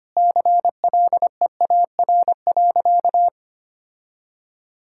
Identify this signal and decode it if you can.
Morse code, heard as a keyed tone, about 25 wpm, 700 Hz. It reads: CLEAR.
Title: Sentences from Top 1000 Words - Review Part 6 - 25wpm